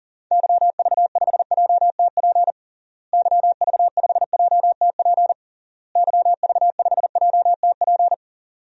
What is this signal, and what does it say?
Morse code, heard as a keyed tone, about 40 wpm, 700 Hz. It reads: YV5JTP YV5JTP YV5JTP
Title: DXCC-100 Call Signs - 6 Characters - Repeated 3x - 40wpm